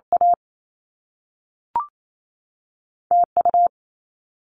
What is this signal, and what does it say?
Morse code, heard as a keyed tone, about 28 wpm, spaced as for 3 wpm, 700 Hz. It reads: AT U
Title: Top Two Letter Combinations in Words 28wpm